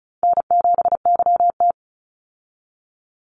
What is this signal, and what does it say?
Morse code, heard as a keyed tone, about 34 wpm, 700 Hz. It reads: N7YT
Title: Easy Call Signs Repeated 3x - 35wpm